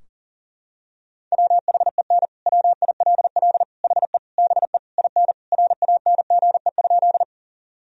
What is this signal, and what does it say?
Morse code, heard as a keyed tone, about 40 wpm, 700 Hz. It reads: WHEN WILL HE BE IN RANGE?
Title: Sentences from Top 1000 Words - Review Part 4 - 40wpm